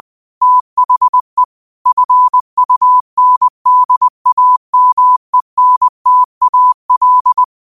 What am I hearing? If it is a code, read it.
Morse code, heard as a keyed tone, about 20 wpm, 1000 Hz. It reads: THE FUNDAMENTAL